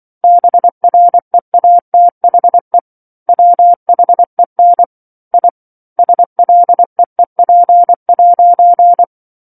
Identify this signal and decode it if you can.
Morse code, heard as a keyed tone, about 24 wpm, 700 Hz. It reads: BREATHE WHEN I SLEEP'
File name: 07-Chapter-07 24 WPM